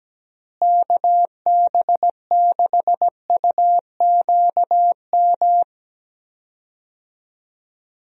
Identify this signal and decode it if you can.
Morse code, heard as a keyed tone, about 17 wpm, 700 Hz. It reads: KB6UQM
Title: USA Call Signs - 6 Characters - Repeated 2x - 17wpm